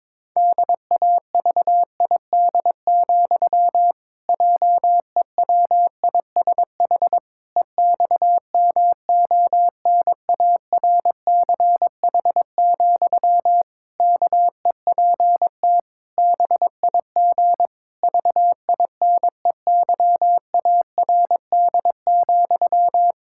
Morse code, heard as a keyed tone, about 22 wpm, 700 Hz. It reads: DAVID, JEWISH EXMONARCH, KEPT BIG VINEYARD,
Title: Pangram Challenge v2 - 22wpm